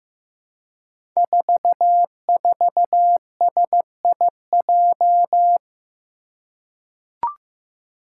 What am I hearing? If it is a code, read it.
Morse code, heard as a keyed tone, about 15 wpm, 700 Hz. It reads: 44SIJ
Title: Five Challenging Letter-Number Combinations 15wpm